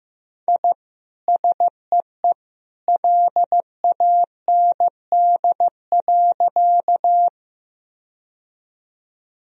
Morse code, heard as a keyed tone, about 15 wpm, 700 Hz. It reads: I SEE LAND.
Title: Sentences from Top 1000 Words - Review Part 1 - 15wpm